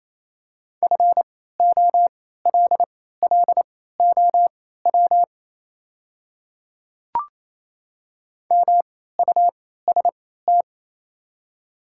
Morse code, heard as a keyed tone, about 28 wpm, spaced as for 16 wpm, 700 Hz. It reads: FOLLOW MUST
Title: 3x Character Spacing - Top 200 Words - 28wpm